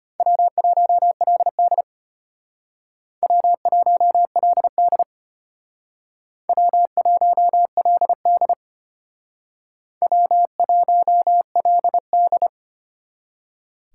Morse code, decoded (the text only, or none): W1LD W1LD W1LD W1LD